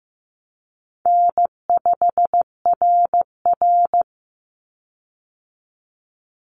Morse code, heard as a keyed tone, about 15 wpm, 700 Hz. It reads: N5RR
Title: Easy Call Signs - 15wpm